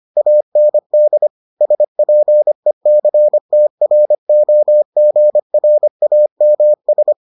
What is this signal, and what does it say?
Morse code, heard as a keyed tone, about 25 wpm, 600 Hz. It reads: AND SPECTROGRAMS